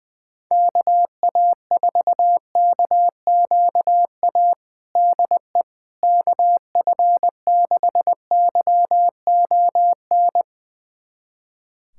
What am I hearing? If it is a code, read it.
Morse code, heard as a keyed tone, about 20 wpm, 700 Hz. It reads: KA4KQA DE KF6YON
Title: USA Callsign DE Callsign 20wpm